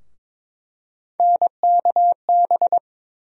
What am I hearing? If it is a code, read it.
Morse code, heard as a keyed tone, about 22 wpm, 700 Hz. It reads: NKB
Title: USA Call Sign Postfixes - 3 Letters 22wpm